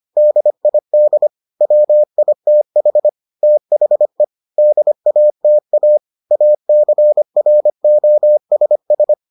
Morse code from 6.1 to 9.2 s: ACROSS

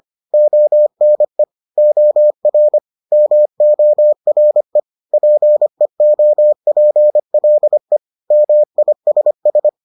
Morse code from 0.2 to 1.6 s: ONE

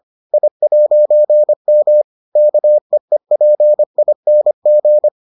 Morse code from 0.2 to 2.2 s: I'M

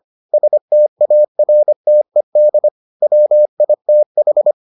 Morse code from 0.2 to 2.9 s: STARTED

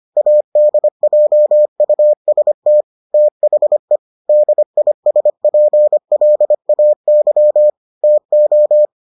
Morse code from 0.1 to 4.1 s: ADJUST THE

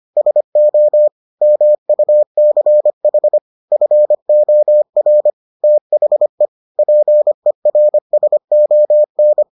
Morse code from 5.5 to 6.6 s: THE

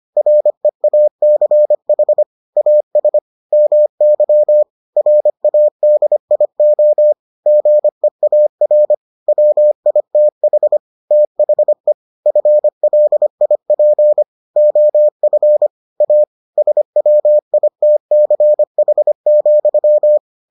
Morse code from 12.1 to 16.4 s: FLIP OF A

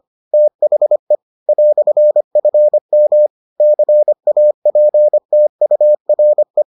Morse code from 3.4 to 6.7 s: CAPTURE

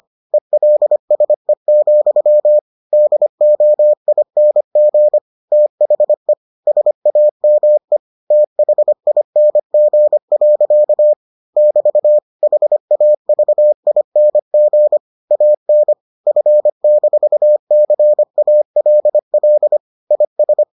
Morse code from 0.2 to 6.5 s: ELSE, DOING THE

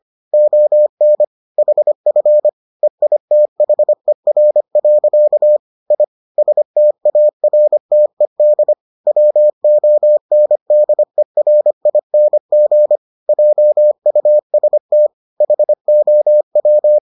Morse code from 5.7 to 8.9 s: I STARTED

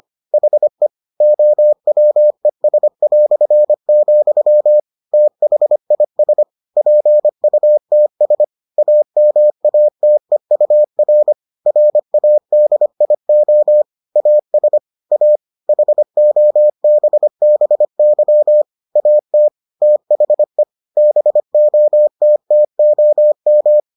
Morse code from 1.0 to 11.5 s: OWES", THIS PUTS AMATEUR

